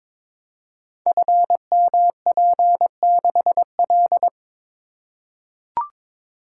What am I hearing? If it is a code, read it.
Morse code, heard as a keyed tone, about 22 wpm, 700 Hz. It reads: FMP6L
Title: Code Groups - 5 Letter:Numbers - 22wpm